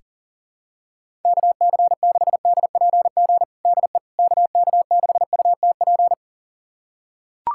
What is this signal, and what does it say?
Morse code, heard as a keyed tone, about 40 wpm, 700 Hz. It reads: KC6DPG DE KK6UTP
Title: USA Callsign DE Callsign - 6 Characters - 40wpm